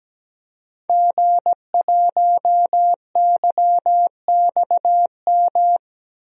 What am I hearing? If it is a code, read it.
Morse code, heard as a keyed tone, about 17 wpm, 700 Hz. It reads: G1YXM
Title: DXCC-100 Call Signs - 5 Characters - 17wpm